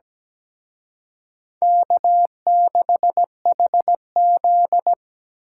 Morse code, heard as a keyed tone, about 20 wpm, 700 Hz. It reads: K6HZ
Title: Speed Racing - USA 1x2 Call Signs 17wpm